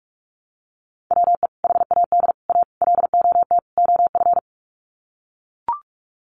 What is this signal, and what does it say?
Morse code, heard as a keyed tone, about 44 wpm, 700 Hz. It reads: WE HAD A LOT OF
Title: Sets of 5 Words Encore 45wpm